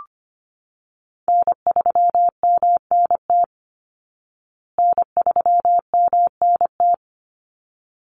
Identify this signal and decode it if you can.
Morse code, heard as a keyed tone, about 25 wpm, 700 Hz. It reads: N3MNT N3MNT
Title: Call Signs Repeated 3x - 25wpm